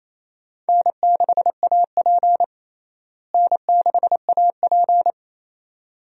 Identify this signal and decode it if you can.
Morse code, heard as a keyed tone, about 28 wpm, 700 Hz. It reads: N6AP N6AP
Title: USA Call Signs - 4 Characters - Repeated 2x - 28wpm